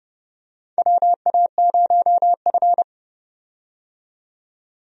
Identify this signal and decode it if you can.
Morse code, heard as a keyed tone, about 32 wpm, 700 Hz. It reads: WA0F
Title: USA Call Signs - 4 Characters - 30wpm